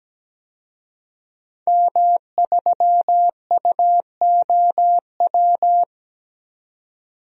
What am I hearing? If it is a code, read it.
Morse code, heard as a keyed tone, about 17 wpm, 700 Hz. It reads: M3UOW
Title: DXCC-100 Call Signs - 5 Characters - 17wpm